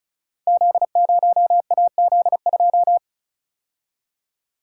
Morse code, heard as a keyed tone, about 36 wpm, 700 Hz. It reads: Z0AZ2
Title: Code Groups - 5 Letter:Numbers - 35wpm